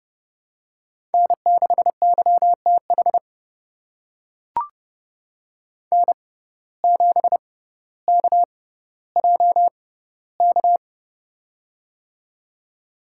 Morse code, heard as a keyed tone, about 30 wpm, spaced as for 15 wpm, 700 Hz. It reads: N6YTH N7KJK
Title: USA Call Signs - 5 Characters - 6x Character Spacing - 30wpm